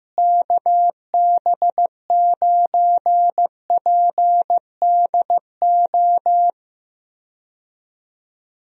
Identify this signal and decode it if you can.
Morse code, heard as a keyed tone, about 15 wpm, 700 Hz. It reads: KB9PDO
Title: USA Call Signs - 6 Characters - Repeated 3x - 15wpm